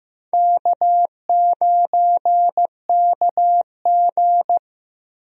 Morse code, heard as a keyed tone, about 15 wpm, 700 Hz. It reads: K9KG
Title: USA Call Signs - 4 Characters - Repeated 3x - 15wpm